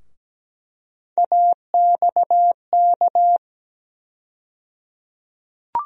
Morse code, heard as a keyed tone, about 17 wpm, 700 Hz. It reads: AXK
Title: USA Call Sign Postfixes - 3 Letters 17wpm